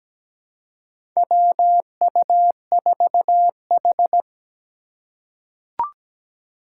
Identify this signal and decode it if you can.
Morse code, heard as a keyed tone, about 17 wpm, 700 Hz. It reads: WU4H
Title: USA Call Signs - 4 Characters - 4x Character Spacing - 17wpm